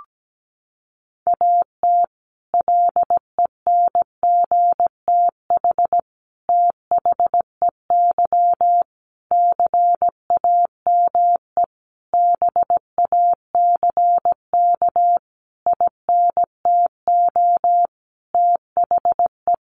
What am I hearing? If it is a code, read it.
Morse code, heard as a keyed tone, about 17 wpm, 700 Hz. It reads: AT LENGTH THEY CAME BACK INTO THE
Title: Sentences from Top 1000 Words 17wpm